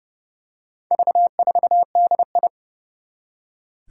Morse code, heard as a keyed tone, about 30 wpm, 700 Hz. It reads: V4DI
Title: Four Challenging Letter-Number Combinations 30wpm